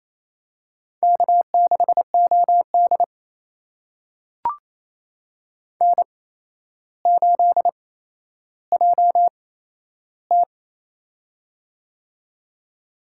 Morse code, heard as a keyed tone, about 29 wpm, spaced as for 12 wpm, 700 Hz. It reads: K6OD N8JT
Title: USA Call Signs - 4 Characters - 8x Character Spacing - 28wpm